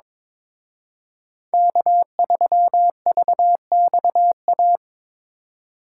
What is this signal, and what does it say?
Morse code, heard as a keyed tone, about 26 wpm, 700 Hz. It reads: K3VXA
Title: Speed Racing - USA 1x3 Call Signs 22wpm